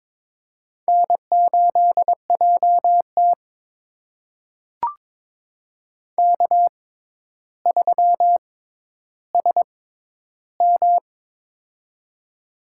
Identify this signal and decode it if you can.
Morse code, heard as a keyed tone, about 22 wpm, spaced as for 11 wpm, 700 Hz. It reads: N8JT K3SM
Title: USA Call Signs - 4 Characters - 6x Character Spacing - 22wpm